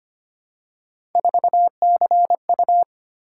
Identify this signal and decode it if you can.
Morse code, heard as a keyed tone, about 25 wpm, 700 Hz. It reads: 4CU
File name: Three Letter-Number Combinations - 25wpm